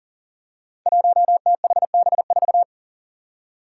Morse code, heard as a keyed tone, about 40 wpm, 700 Hz. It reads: 1THB4
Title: Five Challenging Letter-Number Combinations 40wpm